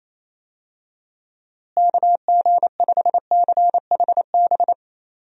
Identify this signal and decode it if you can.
Morse code, heard as a keyed tone, about 28 wpm, 700 Hz. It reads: KG5CHB
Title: USA Call Signs - 6 Characters - Repeated 2x - 28wpm